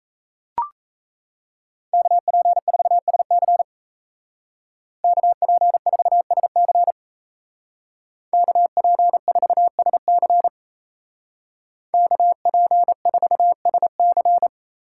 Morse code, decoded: KP4SC KP4SC KP4SC KP4SC

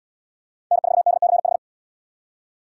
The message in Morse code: N4DCH

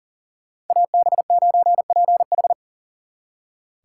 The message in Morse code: AB9PH

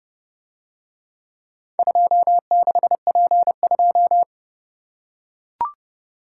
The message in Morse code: 26P2

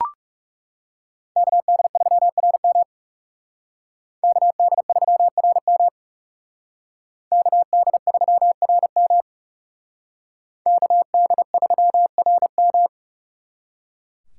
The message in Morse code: KD3RM KD3RM KD3RM KD3RM

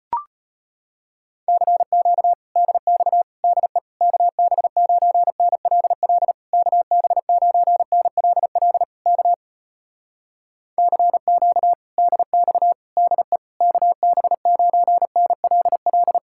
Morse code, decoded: CQ DX DE KB9NLL KB9NLL K CQ DX DE KB9NLL

